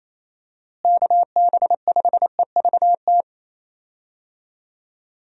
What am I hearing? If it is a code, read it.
Morse code, heard as a keyed tone, about 28 wpm, 700 Hz. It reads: KB5EVT
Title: USA Call Signs - 6 Characters - Repeated 3x - 28wpm